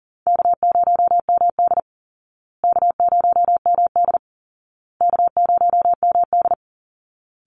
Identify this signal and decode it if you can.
Morse code, heard as a keyed tone, about 40 wpm, 700 Hz. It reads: K0MD K0MD K0MD
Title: Call Signs Repeated 2x - 40wpm